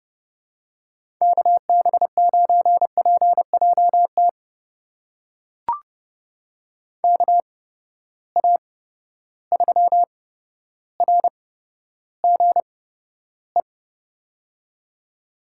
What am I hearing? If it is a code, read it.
Morse code, heard as a keyed tone, about 30 wpm, spaced as for 13 wpm, 700 Hz. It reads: KB9PJT KA3RGE
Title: USA Call Signs - 6 Characters - 8x Character Spacing - 30wpm